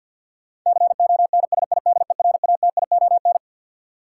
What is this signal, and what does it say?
Morse code, heard as a keyed tone, about 52 wpm, 700 Hz. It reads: CONSIDERATION
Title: Sentences from Top 1500 Words - New Word Drill Alt - 50wpm